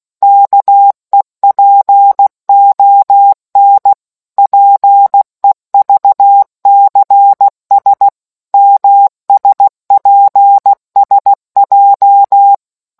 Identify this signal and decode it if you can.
Morse code, heard as a keyed tone, about 16 wpm, 800 Hz. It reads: KEPON PEVCS MSPSJ